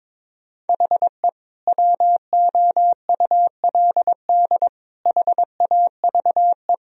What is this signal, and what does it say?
Morse code, heard as a keyed tone, about 22 wpm, 700 Hz. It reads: HE WOULD HAVE
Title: Sentences from Top 1000 Words - Review Part 1 - 22wpm